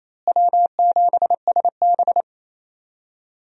Morse code, heard as a keyed tone, about 28 wpm, 700 Hz. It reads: W7SB
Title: USA Call Signs - 4 Characters - Repeated 2x - 28wpm